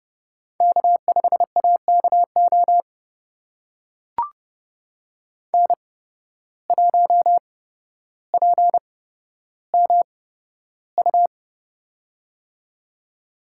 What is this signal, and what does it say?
Morse code, heard as a keyed tone, about 31 wpm, spaced as for 13 wpm, 700 Hz. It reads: K5AKO N1PMU
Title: USA Call Signs - 5 Characters - 8x Character Spacing - 30wpm